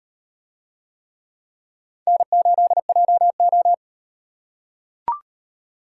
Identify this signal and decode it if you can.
Morse code, heard as a keyed tone, about 35 wpm, 700 Hz. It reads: N8JO